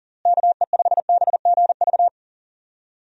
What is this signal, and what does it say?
Morse code, heard as a keyed tone, about 37 wpm, 700 Hz. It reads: KE5BGV
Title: USA Call Signs - 6 Characters - Repeated 3x - 40wpm